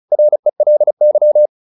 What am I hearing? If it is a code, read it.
Morse code, heard as a keyed tone, about 36 wpm, 600 Hz. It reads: RELY